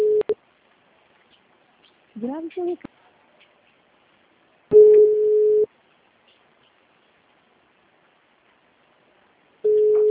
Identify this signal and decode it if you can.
Ringback tone (Europe, 425 Hz), with speech.